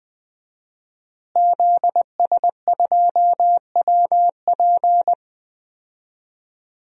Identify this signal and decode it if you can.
Morse code, heard as a keyed tone, about 21 wpm, 700 Hz. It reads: ZS2WP
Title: DXCC-100 Call Signs - 5 Characters - Repeated 3x - 20wpm